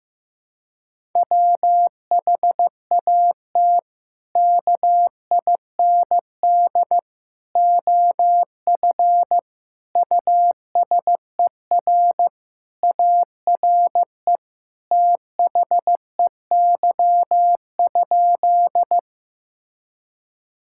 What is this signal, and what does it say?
Morse code, heard as a keyed tone, about 16 wpm, 700 Hz. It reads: WHAT KIND OF USER ARE THEY?
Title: Sentences from Top 1000 Words - Review Part 5 - 15wpm